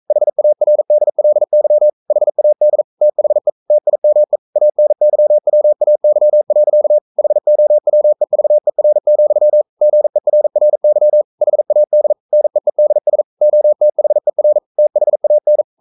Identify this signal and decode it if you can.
Morse code, heard as a keyed tone, about 42 wpm, 600 Hz. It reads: HARDLY HAD THE TIME ANYWAY. HOWEVER, GERRY HAD NEEDS OTHER THAN
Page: Romanen Never af Ken Follet på engelsk.